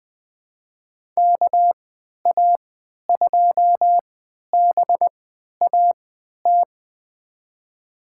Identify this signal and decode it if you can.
Morse code, heard as a keyed tone, about 20 wpm, spaced as for 11 wpm, 700 Hz. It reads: KA2BAT
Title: USA Call Signs - 6 Characters - 3x Character Spacing - 20wpm